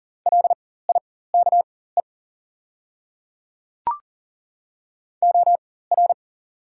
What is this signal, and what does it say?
Morse code, heard as a keyed tone, about 40 wpm, spaced as for 19 wpm, 700 Hz. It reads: LIKE OR
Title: 4x Character Spacing - Top 100 Words - 40wpm